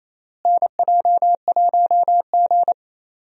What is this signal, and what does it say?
Morse code, heard as a keyed tone, about 28 wpm, 700 Hz. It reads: NJ1G
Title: USA Call Signs - 4 Characters - Repeated 2x - 28wpm